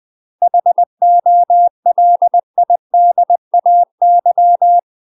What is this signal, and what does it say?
Morse code, heard as a keyed tone, about 20 wpm, 700 Hz. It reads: HOLIDAY
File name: UK MASTER QSOs TRAINING/20WPM